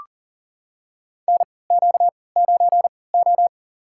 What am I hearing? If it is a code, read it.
Morse code, heard as a keyed tone, about 37 wpm, spaced as for 22 wpm, 700 Hz. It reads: NQ9O
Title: USA Call Signs - 3x Character Spacing - 40wpm